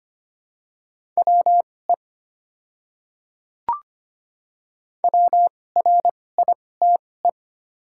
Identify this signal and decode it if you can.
Morse code, heard as a keyed tone, about 25 wpm, spaced as for 18 wpm, 690 Hz. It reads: WE WRITE
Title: 2x Character Spacing - Top 100 Words - 25wpm